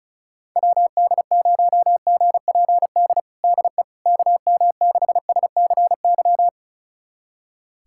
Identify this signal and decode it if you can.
Morse code, heard as a keyed tone, about 36 wpm, 690 Hz. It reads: WD0GPD DE KM6SCY
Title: USA Callsign DE Callsign 35wpm